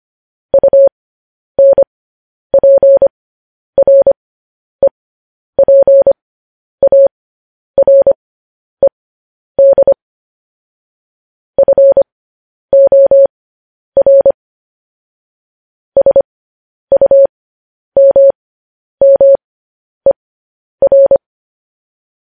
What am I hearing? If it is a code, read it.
Morse code, heard as a keyed tone, about 25 wpm, spaced as for 10 wpm, 550 Hz. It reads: UNPREPARED FOR SUMMER
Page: news headlines in morse code at 10 wpm